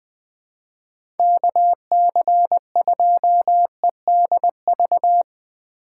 Morse code, heard as a keyed tone, about 20 wpm, 700 Hz. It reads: KC2EDV
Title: USA Call Signs - 6 Characters - 3x Character Spacing - 20wpm